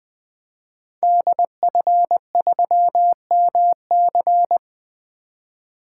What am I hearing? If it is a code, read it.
Morse code, heard as a keyed tone, about 20 wpm, 700 Hz. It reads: DF3MC
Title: 2023 SOTA Activator Call Signs - 20wpm